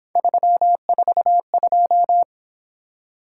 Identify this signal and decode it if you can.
Morse code, heard as a keyed tone, about 26 wpm, 700 Hz. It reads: 342
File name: Speed-Racing - Three Letter-Number Combinations - 22wpm